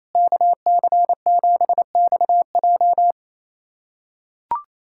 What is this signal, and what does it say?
Morse code, heard as a keyed tone, about 28 wpm, 700 Hz. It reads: KC7XJ
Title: USA Callsign DE Callsign - 5 Characters - 28wpm